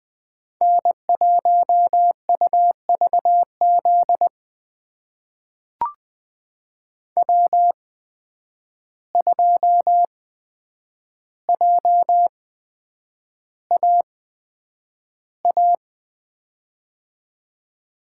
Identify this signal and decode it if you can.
Morse code, heard as a keyed tone, about 20 wpm, spaced as for 9 wpm, 700 Hz. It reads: N1UVZ W2JAA
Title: USA Call Signs - 5 Characters - 8x Character Spacing - 20wpm